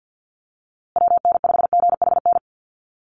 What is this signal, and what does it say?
Morse code, heard as a keyed tone, about 50 wpm, 700 Hz. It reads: WN5GHN
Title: USA Call Signs Repeated 2x 50wpm